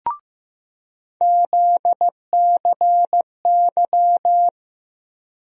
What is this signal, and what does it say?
Morse code, heard as a keyed tone, about 15 wpm, 700 Hz. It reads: ZCY
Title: USA Call Sign Postfixes - 3 Letters 15wpm